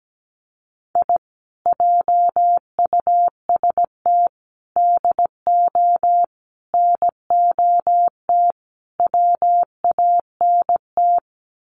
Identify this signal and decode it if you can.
Morse code, heard as a keyed tone, about 17 wpm, 700 Hz. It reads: I JUST DO NOT WANT
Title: Sets of 5 Words Encore2 17wpm